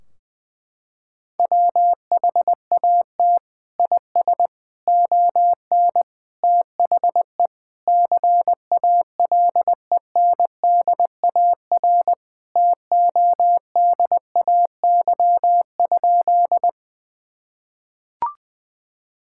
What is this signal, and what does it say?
Morse code, heard as a keyed tone, about 20 wpm, 700 Hz. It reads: WHAT IS ON THE CALENDAR TODAY?
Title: Sentences from Top 1600 Words 20wpm